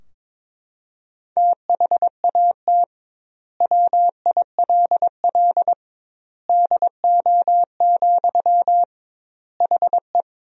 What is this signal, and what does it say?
Morse code, heard as a keyed tone, about 22 wpm, 700 Hz. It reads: THAT WILL DO, HE
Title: 2x Word-Spacing - Sentences from Top 100 Words - 22wpm